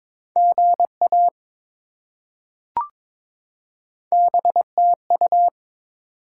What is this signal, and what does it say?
Morse code, heard as a keyed tone, about 21 wpm, 700 Hz. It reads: GA BTU
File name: practice / 22wpm / Top 20 QSO Elements - 22wpm